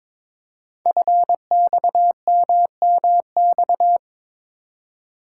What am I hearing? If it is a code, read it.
Morse code, heard as a keyed tone, about 22 wpm, 700 Hz. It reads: FXMMX
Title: Code Groups - 5 Letters - 22wpm